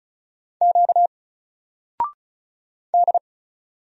Morse code, heard as a keyed tone, about 33 wpm, 700 Hz. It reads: Q D
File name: Single-Letter-Number-35wpm